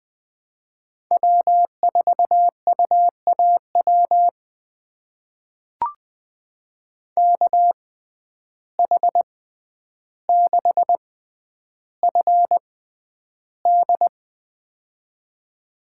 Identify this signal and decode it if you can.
Morse code, heard as a keyed tone, about 20 wpm, spaced as for 10 wpm, 700 Hz. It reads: W4UAW KH6FD
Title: USA Call Signs - 5 Characters - 6x Character Spacing - 20wpm